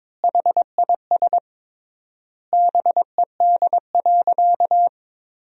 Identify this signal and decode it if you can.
Morse code, heard as a keyed tone, about 22 wpm, 700 Hz. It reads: HIS BED.
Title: 3x Word-Spacing - Sentences from Top 500 Words - 22wpm